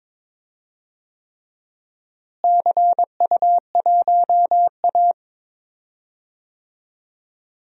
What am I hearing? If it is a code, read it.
Morse code, heard as a keyed tone, about 22 wpm, 700 Hz. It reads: CU1A
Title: DXCC-100 Call Signs - 4 Characters - Repeated 3x - 22wpm